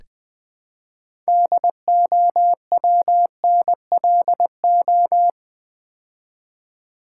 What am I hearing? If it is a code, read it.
Morse code, heard as a keyed tone, about 20 wpm, 700 Hz. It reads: DOWNLO
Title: Build A Word - 2 Letters At A Time - 8-Letter Words - 20wpm